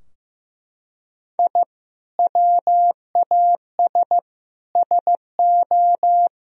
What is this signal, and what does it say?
Morse code, heard as a keyed tone, about 15 wpm, 700 Hz. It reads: I WAS SO
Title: Sentences from Top 1000 Words - Review Part 4 - 15wpm